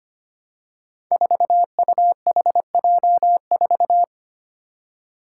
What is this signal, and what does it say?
Morse code, heard as a keyed tone, about 25 wpm, 700 Hz. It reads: 4UHJ4
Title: Five Challenging Letter-Number Combinations 25wpm